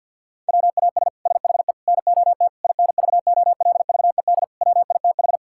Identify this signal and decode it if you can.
Morse code, heard as a keyed tone, about 50 wpm, 700 Hz. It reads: WAS SHE NOT INVOLVED WITH
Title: Sentences from Top 1000 Words - Review Part 5 - 50wpm